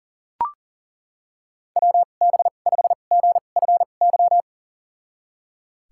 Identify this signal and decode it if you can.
Morse code, heard as a keyed tone, about 40 wpm, spaced as for 29 wpm, 700 Hz. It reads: WB5GFY
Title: USA Call Signs - 6 Characters - 2x Character Spacing - 40wpm